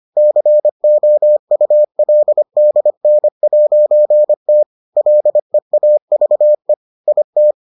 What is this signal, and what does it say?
Morse code, heard as a keyed tone, about 25 wpm, 600 Hz. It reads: COULDN'T LEAVE IT